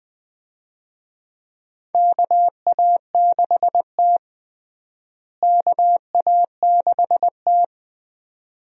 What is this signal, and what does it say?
Morse code, heard as a keyed tone, about 20 wpm, 700 Hz. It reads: KA6T KA6T
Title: USA Call Signs - 4 Characters - Repeated 2x - 20wpm